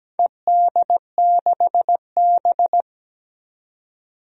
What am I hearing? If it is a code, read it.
Morse code, heard as a keyed tone, about 17 wpm, 700 Hz. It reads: ED6B
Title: DXCC-100 Call Signs - 4 Characters - 17wpm